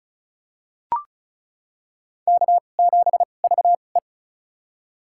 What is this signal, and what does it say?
Morse code, heard as a keyed tone, about 35 wpm, spaced as for 25 wpm, 700 Hz. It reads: K7VE
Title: USA Call Signs - 4 Characters - 2x Character Spacing - 35wpm